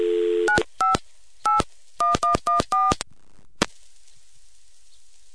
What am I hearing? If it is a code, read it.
DTMF, heard as telephone touch tones, tones of about 150 ms. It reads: 95 81114